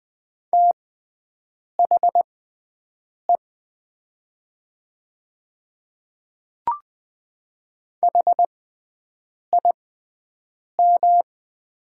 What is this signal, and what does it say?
Morse code, heard as a keyed tone, about 20 wpm, spaced as for 7 wpm, 700 Hz. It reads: THE HIM
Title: 6x Character Spacing - Top 100 Words - 20wpm